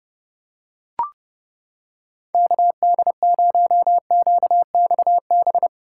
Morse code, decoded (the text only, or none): KD0QXB